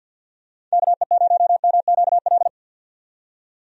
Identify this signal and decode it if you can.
Morse code, heard as a keyed tone, about 51 wpm, 700 Hz. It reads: KE0MQL
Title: USA Call Signs - 6 Characters - 50wpm